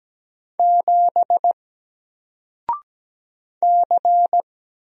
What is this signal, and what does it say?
Morse code, heard as a keyed tone, about 17 wpm, 700 Hz. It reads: 7 C
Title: Single Character (Letter:Number:Punctuation:Prosign) Alt 17wpm